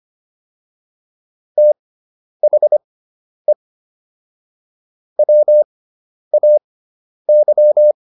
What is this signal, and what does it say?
Morse code, heard as a keyed tone, about 25 wpm, spaced as for 10 wpm, 600 Hz. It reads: THE WAY